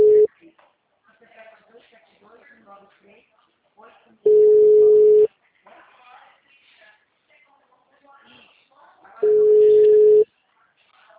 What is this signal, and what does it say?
Ringback tone (Europe, 425 Hz), with speech.